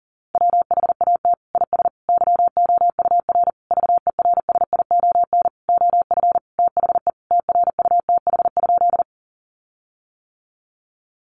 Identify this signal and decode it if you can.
Morse code, heard as a keyed tone, about 40 wpm, 700 Hz. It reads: WHAT IS YOUR VERSION OF THE TRUTH?
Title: Sentences from Top 1400 Words 40wpm